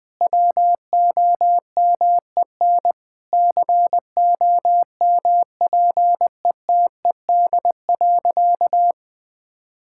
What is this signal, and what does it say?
Morse code, heard as a keyed tone, about 20 wpm, 700 Hz. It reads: WOMEN COMPETED.
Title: Sentences with 3-Word Phrases - 20wpm